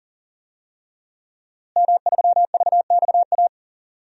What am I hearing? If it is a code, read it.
Morse code, heard as a keyed tone, about 40 wpm, 700 Hz. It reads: M3VXA
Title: DXCC-100 Call Signs - 5 Characters - Repeated 2x - 40wpm